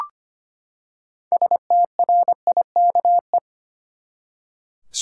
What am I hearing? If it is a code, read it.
Morse code, heard as a keyed tone, about 25 wpm, 700 Hz. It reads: STRIKE
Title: Six Letter Words - 25wpm